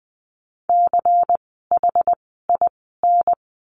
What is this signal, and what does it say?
Morse code, heard as a keyed tone, about 20 wpm, spaced as for 15 wpm, 700 Hz. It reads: CHIN
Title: 094 Lesson 16f - Letter c - Words 20wpm